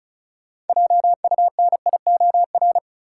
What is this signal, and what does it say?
Morse code, heard as a keyed tone, about 34 wpm, 700 Hz. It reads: JUNIOR
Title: Sentences from Top 1000 Words - Review Part 7 - New Word Drill - 35wpm